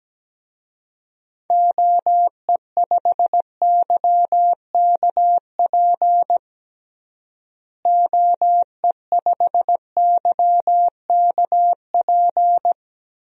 Morse code, heard as a keyed tone, about 17 wpm, 700 Hz. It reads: OE5YKP OE5YKP
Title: DXCC-100 Call Signs - 6 Characters - Repeated 2x - 17wpm